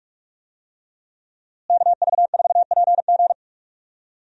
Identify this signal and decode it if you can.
Morse code, heard as a keyed tone, about 43 wpm, 700 Hz. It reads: KV4PG
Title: USA Call Signs - 5 Characters - Repeated 3x - 45wpm